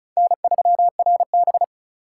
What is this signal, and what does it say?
Morse code, heard as a keyed tone, about 34 wpm, 700 Hz. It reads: N3RB
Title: USA Call Signs - 4 Characters - 3x Character Spacing - 35wpm